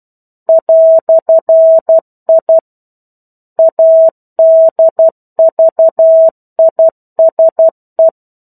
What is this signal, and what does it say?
Morse code, heard as a keyed tone, about 12 wpm, 650 Hz. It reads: "I ADVISE
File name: 01-Chapter-01 12 WPM